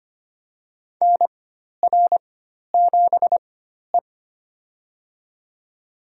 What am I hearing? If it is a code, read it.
Morse code, heard as a keyed tone, about 25 wpm, spaced as for 12 wpm, 700 Hz. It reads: NR7E
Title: USA Call Signs - 4x Character Spacing - 25wpm